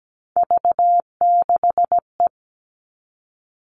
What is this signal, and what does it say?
Morse code, heard as a keyed tone, about 17 wpm, 700 Hz. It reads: V6E
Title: All Cut Numbers - 3-Digit 17wpm